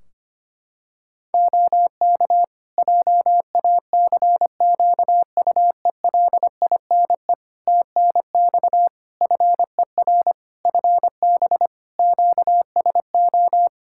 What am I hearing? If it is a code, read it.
Morse code, heard as a keyed tone, about 25 wpm, 700 Hz. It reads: OK JACQUELINE TNX FER FB QSO
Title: K1USN SST QSO Practice 25wpm